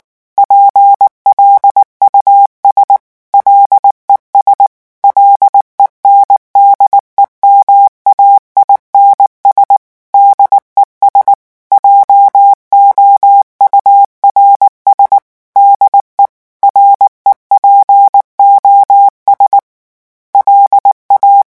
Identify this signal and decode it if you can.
Morse code, heard as a keyed tone, about 19 wpm, 800 Hz. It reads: PLUS LES LENDEMAINS DES JOURS DE REPOS LA
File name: morse mp3 / revision9 citations humoristiques vitesse 18 mots minute